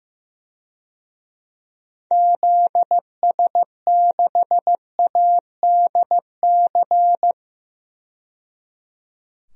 Morse code, heard as a keyed tone, about 15 wpm, 700 Hz. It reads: ZS6ADC
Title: DXCC-100 Call Signs - 6 Characters - 15wpm